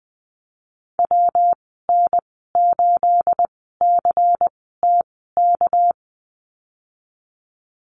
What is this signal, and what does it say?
Morse code, heard as a keyed tone, about 20 wpm, spaced as for 15 wpm, 700 Hz. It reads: WN8CTK
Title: 219 Lesson 35g - Number 0 - Call Signs 20wpm